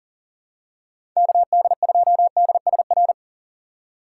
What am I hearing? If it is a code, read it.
Morse code, heard as a keyed tone, about 40 wpm, 700 Hz. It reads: KD2DSR
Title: USA Call Signs - 6 Characters - 6x Character Spacing - 40wpm